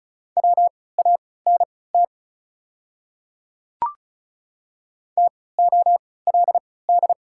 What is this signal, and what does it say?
Morse code, heard as a keyed tone, about 36 wpm, spaced as for 20 wpm, 690 Hz. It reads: WANT TOLD
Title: 3x Character Spacing - Top 500 Words - 35wpm